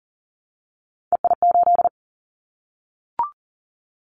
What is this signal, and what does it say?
Morse code, heard as a keyed tone, about 42 wpm, 700 Hz. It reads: EI8
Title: Callsign Prefixes for Top 50 DXCC - 40wpm